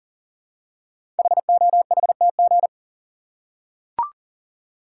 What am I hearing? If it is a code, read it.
Morse code, heard as a keyed tone, about 40 wpm, 690 Hz. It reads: HOHTG
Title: Code Groups - 5 Letters - 40wpm